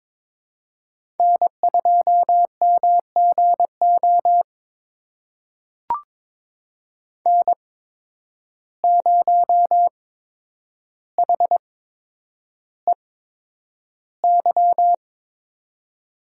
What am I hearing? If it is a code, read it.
Morse code, heard as a keyed tone, about 22 wpm, spaced as for 9 wpm, 700 Hz. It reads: N2MGO N0HEY
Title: USA Call Signs - 5 Characters - 8x Character Spacing - 22wpm